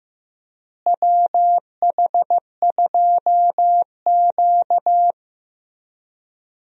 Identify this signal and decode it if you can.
Morse code, heard as a keyed tone, about 15 wpm, 700 Hz. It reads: WH2Q
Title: USA Call Signs - 4 Characters - Repeated 3x - 15wpm